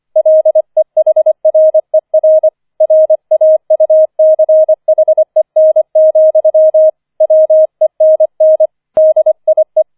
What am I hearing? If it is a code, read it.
Morse code, heard as a keyed tone, about 25 wpm, 630 Hz. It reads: LEHRER RAUCHEN, WENN DIE
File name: 004 Vierter Streich 25wpm